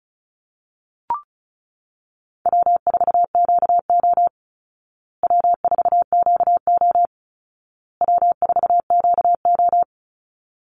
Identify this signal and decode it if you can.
Morse code, heard as a keyed tone, about 34 wpm, 700 Hz. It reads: W4QO W4QO W4QO
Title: Easy Call Signs Repeated 3x - 35wpm